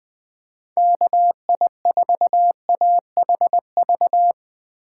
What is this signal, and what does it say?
Morse code, heard as a keyed tone, about 20 wpm, 700 Hz. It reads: KI4AHV
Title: USA Call Signs - 6 Characters - 2x Character Spacing - 20wpm